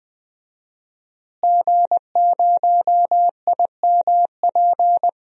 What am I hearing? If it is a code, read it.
Morse code, heard as a keyed tone, about 20 wpm, 700 Hz. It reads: G0IMP
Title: DXCC-100 Call Signs - 5 Characters - Repeated 2x - 20wpm